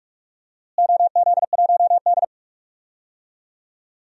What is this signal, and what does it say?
Morse code, heard as a keyed tone, about 43 wpm, 700 Hz. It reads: OZ1D
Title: DXCC-100 Call Signs - 4 Characters - Repeated 2x - 45wpm